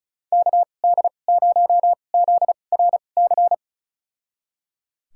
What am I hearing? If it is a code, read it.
Morse code, heard as a keyed tone, about 36 wpm, spaced as for 25 wpm, 700 Hz. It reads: KD0ZRC
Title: USA Call Signs - 6 Characters - 2x Character Spacing - 35wpm